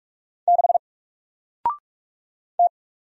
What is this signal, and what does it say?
Morse code, heard as a keyed tone, about 44 wpm, 700 Hz. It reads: 6 T
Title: Single Letter-Number 45wpm